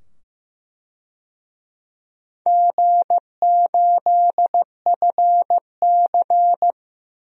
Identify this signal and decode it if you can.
Morse code, heard as a keyed tone, about 15 wpm, 700 Hz. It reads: G8FC
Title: DXCC-100 Call Signs - 4 Characters - 15wpm